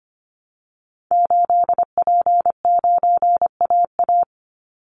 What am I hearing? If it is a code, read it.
Morse code, heard as a keyed tone, about 25 wpm, 700 Hz. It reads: 8P9AA
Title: Call Signs Repeated 3x - 25wpm